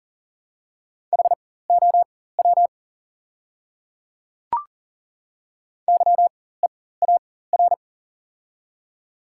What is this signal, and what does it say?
Morse code, heard as a keyed tone, about 38 wpm, spaced as for 19 wpm, 690 Hz. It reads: HOW YEAR
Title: 4x Character Spacing - Top 300 Words - 40wpm